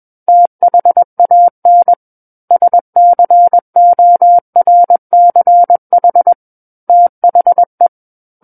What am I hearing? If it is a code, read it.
Morse code, heard as a keyed tone, about 21 wpm, 700 Hz. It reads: THAN SCORCH THE
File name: Book One / 12-Chapter 12 - 21 WPM